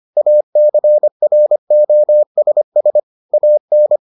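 Morse code, heard as a keyed tone, about 25 wpm, 600 Hz. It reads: ACROSS AN
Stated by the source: Morse edition of the Foundations of Amateur Radio episodes.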